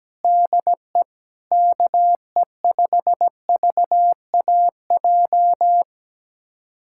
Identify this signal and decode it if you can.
Morse code, heard as a keyed tone, about 17 wpm, 700 Hz. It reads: DE KE5VAJ
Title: USA Callsign DE Callsign - 6 Characters - 17wpm